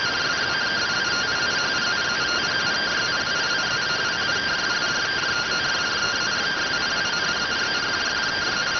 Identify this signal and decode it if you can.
Two-tone teleprinter signal, about 170 Hz shift.